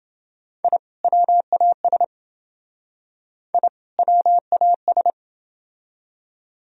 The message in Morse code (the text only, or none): I WAS I WAS